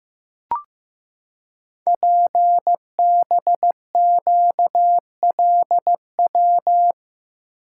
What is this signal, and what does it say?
Morse code, heard as a keyed tone, about 15 wpm, 700 Hz. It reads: PBQLW
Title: Code Groups - 5 Letters - 15wpm